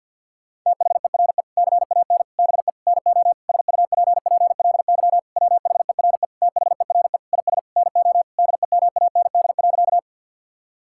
Morse code, heard as a keyed tone, about 49 wpm, 690 Hz. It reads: THERE CAN BE NO SUPPLY WHERE THERE IS NO DEMAND.
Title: Sentences from Top 1000 Words - Review Part 5 - 50wpm